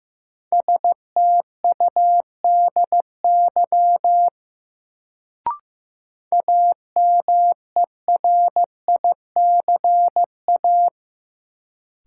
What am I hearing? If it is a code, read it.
Morse code, heard as a keyed tone, about 15 wpm, 700 Hz. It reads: STUDY AMERICA
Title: Sets of 5 Words - New Word Drill - 15wpm